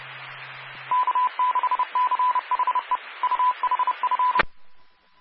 Morse code, heard as a keyed tone, about 30 wpm, 1000 Hz, with radio noise. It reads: K6CHE UHF
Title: Claremont, CA Police, Amateur Radio